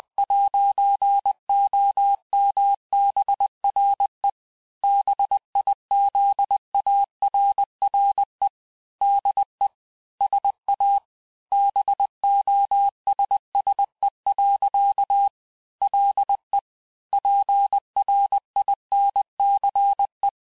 Morse code, heard as a keyed tone, about 20 wpm, 800 Hz. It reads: 'OMBRE BIZARRE DE SA BOSSE. LE PRINCE
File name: cours-du-mercredi / audios / cours-870-texte-CLAIR-vitesse-19-MPM